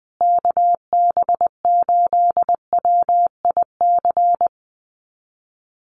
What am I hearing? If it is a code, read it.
Morse code, heard as a keyed tone, about 20 wpm, 700 Hz. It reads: KB8WIC
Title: USA Call Signs Repeated 2x 20wpm